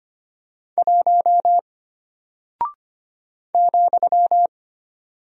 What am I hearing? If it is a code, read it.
Morse code, heard as a keyed tone, about 25 wpm, 700 Hz. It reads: 1 ,
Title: Single Character (Letter:Number:Punctuation:Prosign) Alt 25wpm